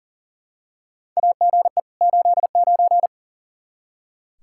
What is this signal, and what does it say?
Morse code, heard as a keyed tone, about 40 wpm, 700 Hz. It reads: AGE 89
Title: K1USN SST QSO Practice 40wpm